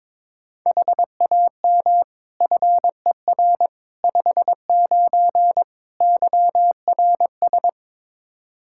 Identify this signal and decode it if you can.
Morse code, heard as a keyed tone, about 22 wpm, 700 Hz. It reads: HAM FER 59 YRS
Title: Practice QSO - Exchange 3 - More Info - 22wpm